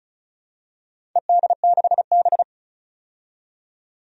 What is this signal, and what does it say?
Morse code, heard as a keyed tone, about 34 wpm, 700 Hz. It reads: ED6B
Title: DXCC-100 Call Signs - 4 Characters - 35wpm